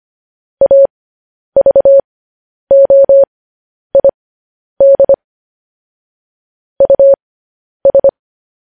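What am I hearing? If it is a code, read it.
Morse code, heard as a keyed tone, about 25 wpm, spaced as for 10 wpm, 550 Hz. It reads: AVOID US
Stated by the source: Morse code news headlines at 10 wpm.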